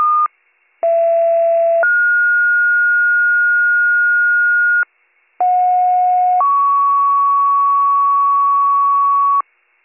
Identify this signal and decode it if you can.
Two-tone page, heard as a one-second tone, then a three-second tone, about 670 then 1430 Hz, about 730 then 1120 Hz, with noise. It is heard from 0.8 to 9.4 s.